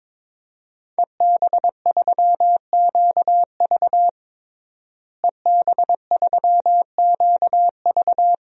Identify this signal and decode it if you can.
Morse code, heard as a keyed tone, about 22 wpm, 700 Hz. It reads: EB3QV EB3QV
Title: DXCC-100 Call Signs - 5 Characters - Repeated 3x - 22wpm